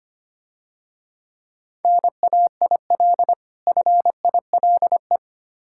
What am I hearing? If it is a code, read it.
Morse code, heard as a keyed tone, about 27 wpm, 700 Hz. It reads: NAIL FILE
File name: practice / 25wpm / Difficult Words - Speed-Racing with Hints - 25wpm